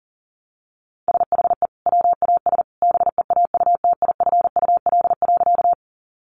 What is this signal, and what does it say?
Morse code, heard as a keyed tone, about 37 wpm, 700 Hz. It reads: SHE WAS BEAUTIFUL.
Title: Sentences from Top 900 Words 40wpm